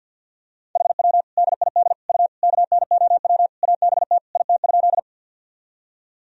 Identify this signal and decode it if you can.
Morse code, heard as a keyed tone, about 49 wpm, 700 Hz. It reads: HW DID U KNOW ABT IT?
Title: Sentences from Top 100 Words and Abbreviations 50wpm